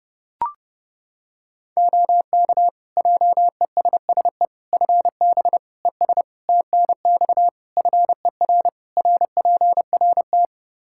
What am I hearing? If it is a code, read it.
Morse code, heard as a keyed tone, about 30 wpm, 700 Hz. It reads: OK JESSE FB ES TNX FER RPRT
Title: Practice QSO - Exchange 2 - Info - 30wpm